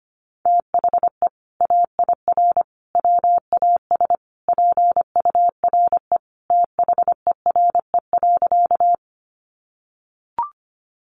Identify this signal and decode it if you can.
Morse code, heard as a keyed tone, about 25 wpm, 700 Hz. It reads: THE AIR WAS PURE THERE.
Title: Sentences from Top 1500 Words 25wpm